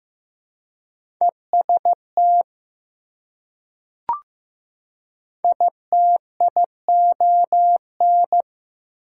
Morse code, heard as a keyed tone, about 15 wpm, 700 Hz. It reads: EST ITION
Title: Prefixes and Suffixes 15wpm